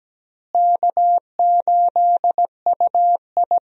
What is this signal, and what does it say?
Morse code, heard as a keyed tone, about 17 wpm, 700 Hz. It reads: K8UI
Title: USA Call Signs - 4 Characters - Repeated 3x - 17wpm